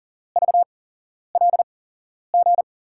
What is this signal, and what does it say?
Morse code, heard as a keyed tone, about 39 wpm, spaced as for 11 wpm, 700 Hz. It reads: VLG
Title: USA Call Signs - 5 Characters - 8x Character Spacing - 40wpm